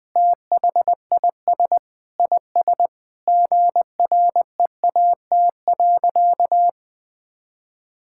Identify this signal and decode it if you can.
Morse code, heard as a keyed tone, about 20 wpm, 700 Hz. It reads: THIS IS GREAT.
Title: Sentences with 5-Word Phrases - 20wpm